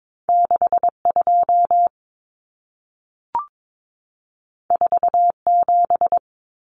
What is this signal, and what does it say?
Morse code, heard as a keyed tone, about 22 wpm, 700 Hz. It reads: 62 47
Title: Numbers - 2-Digit 22wpm